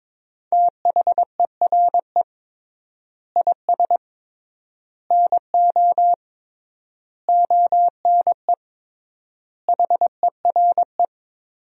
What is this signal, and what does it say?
Morse code, heard as a keyed tone, about 21 wpm, 700 Hz. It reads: THERE IS NO ONE HERE
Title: 3x Word-Spacing - Sentences from Top 200 Words - 22wpm